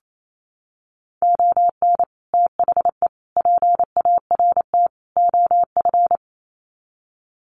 Morse code, heard as a keyed tone, about 28 wpm, 700 Hz. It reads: ON THE PART OF
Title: Sets of 4 Words 28wpm